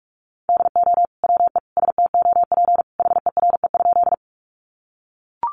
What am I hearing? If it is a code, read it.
Morse code, heard as a keyed tone, about 44 wpm, 700 Hz. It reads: DO WE STOP HERE?
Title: Sentences from Top 300 Words 45wpm